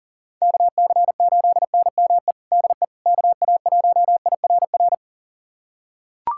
Morse code, heard as a keyed tone, about 40 wpm, 700 Hz. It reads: KC8NME DE KA1IRR
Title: USA Callsign DE Callsign - 6 Characters - 40wpm